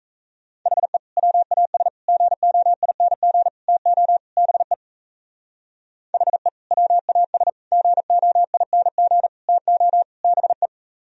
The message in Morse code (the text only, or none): HE WAS GOING TO BE HE WAS GOING TO BE